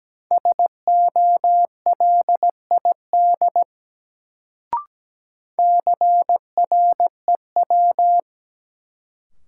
Morse code, heard as a keyed tone, about 17 wpm, 700 Hz. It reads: SOLID CREW
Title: Sentences from Top 1500 Words - New Word Drill Alt - 17wpm